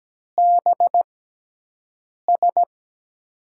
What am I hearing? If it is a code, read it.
Morse code, heard as a keyed tone, about 17 wpm, spaced as for 6 wpm, 700 Hz. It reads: BS
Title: USA Call Signs - 5 Characters - 6x Character Spacing - 17wpm